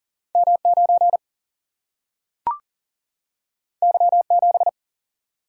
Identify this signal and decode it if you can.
Morse code, heard as a keyed tone, about 39 wpm, 700 Hz. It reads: M9 Y7
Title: Single-Letter Single-Number 40wpm